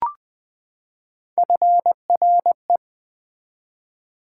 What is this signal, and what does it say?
Morse code, heard as a keyed tone, about 20 wpm, 700 Hz. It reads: FRE
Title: Build A Word - 3 Letters At A Time - 9-Letter Words - 20wpm